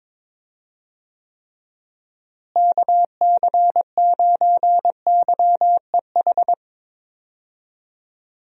Morse code, heard as a keyed tone, about 22 wpm, 700 Hz. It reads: KC9YEH
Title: USA Call Signs - 6 Characters - Repeated 2x - 22wpm